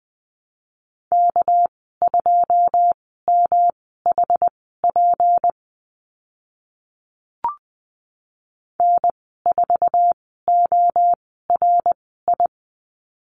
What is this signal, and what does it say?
Morse code, heard as a keyed tone, about 20 wpm, spaced as for 14 wpm, 700 Hz. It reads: K2MHP N4ORI
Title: 207 Lesson 33g - Letter j - Call Signs 20wpm